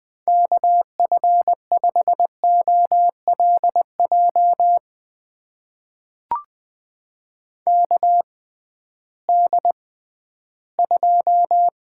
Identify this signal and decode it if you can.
Morse code, heard as a keyed tone, about 20 wpm, spaced as for 13 wpm, 700 Hz. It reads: KF5OLJ KD2
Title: USA Call Signs - 6 Characters - 6x Character Spacing - 20wpm